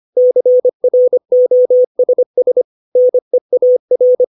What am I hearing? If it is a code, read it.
Morse code, heard as a keyed tone, about 25 wpm, 500 Hz. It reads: CROSS NEAR
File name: 632 words / 632 Words 25WPM